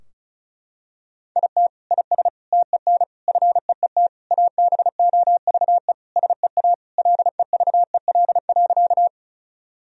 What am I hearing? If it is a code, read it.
Morse code, heard as a keyed tone, about 36 wpm, 700 Hz. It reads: IT IS TEN FEET ABOVE SEA LEVEL.
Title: Sentences from Top 1000 Words - Review Part 4 - 35wpm